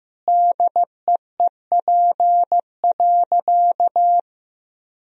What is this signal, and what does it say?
Morse code, heard as a keyed tone, about 14 wpm, 700 Hz. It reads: DEEP.
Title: Sentences from Top 1000 Words - Review Part 3 - 15wpm